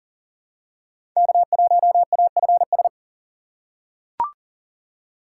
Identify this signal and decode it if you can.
Morse code, heard as a keyed tone, about 40 wpm, 700 Hz. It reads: K1AFS
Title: USA Call Signs - 5 Characters - 3x Character Spacing - 40wpm